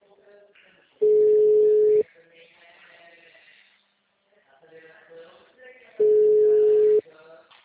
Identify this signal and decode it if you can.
Ringback tone (Europe, 425 Hz), with speech.
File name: out- -2016- - - .94